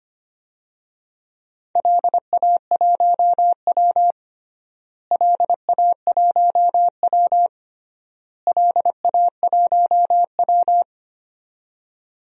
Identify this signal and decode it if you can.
Morse code, heard as a keyed tone, about 25 wpm, 700 Hz. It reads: LA1W LA1W LA1W
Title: DXCC-100 Call Signs - 4 Characters - Repeated 3x - 25wpm